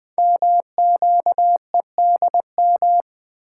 Morse code, heard as a keyed tone, about 20 wpm, 700 Hz. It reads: MQEDM
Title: Code Groups - 5 Letter:Numbers - 20wpm